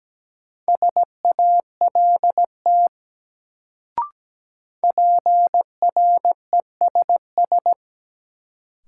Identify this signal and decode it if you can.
Morse code, heard as a keyed tone, about 17 wpm, 700 Hz. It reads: SALT PRESS
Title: Sentences from Top 800 Words - New Word Drill - 17wpm